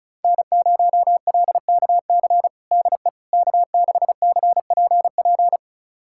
Morse code, heard as a keyed tone, about 36 wpm, 700 Hz. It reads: N0LKC DE K6CPP
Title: USA Callsign DE Callsign - 5 Characters - 35wpm